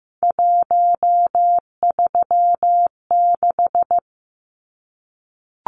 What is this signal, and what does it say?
Morse code, heard as a keyed tone, about 15 wpm, 700 Hz. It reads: 136
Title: Numbers - 3-Digit 15wpm